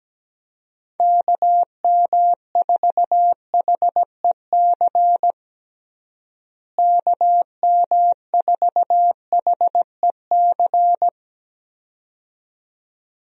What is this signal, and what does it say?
Morse code, heard as a keyed tone, about 17 wpm, 700 Hz. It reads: KM4HEC KM4HEC
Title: USA Call Signs - 6 Characters - Repeated 2x - 17wpm